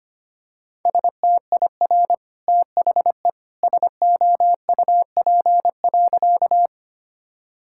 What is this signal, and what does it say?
Morse code, heard as a keyed tone, about 25 wpm, 700 Hz. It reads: STIR THE SOUP.
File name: Easy Sentences with Hints - 25wpm